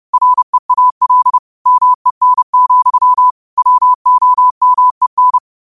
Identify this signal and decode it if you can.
Morse code, heard as a keyed tone, about 30 wpm, 1000 Hz. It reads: REAL MEN, WOMEN
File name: qotdfiles / 30wpm